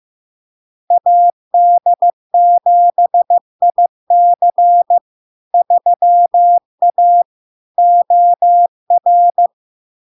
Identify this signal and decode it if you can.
Morse code, heard as a keyed tone, about 15 wpm, 700 Hz. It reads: AD7IC 3A OR
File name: LARK-FD-2022-Log-in-CW-15wpm-700-Hz